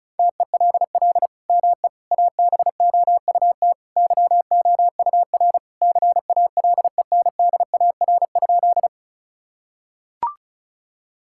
Morse code, heard as a keyed tone, about 36 wpm, 700 Hz. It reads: TELL ME ABOUT YOUR CALENDAR?
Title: Sentences from Top 1600 Words 35wpm